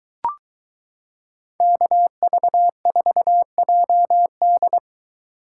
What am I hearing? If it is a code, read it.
Morse code, heard as a keyed tone, about 22 wpm, 700 Hz. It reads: KV4JD